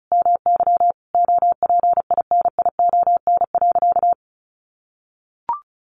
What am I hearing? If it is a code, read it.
Morse code, heard as a keyed tone, about 34 wpm, 700 Hz. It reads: MY OPINION.
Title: Sentences from Top 1200 Words 35wpm